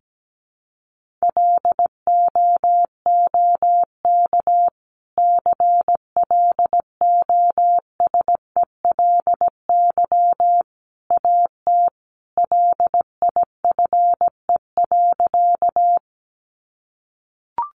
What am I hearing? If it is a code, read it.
Morse code, heard as a keyed tone, about 17 wpm, 700 Hz. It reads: LOOK CLOSELY AT LIFE.
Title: Sentences from Top 1400 Words 17wpm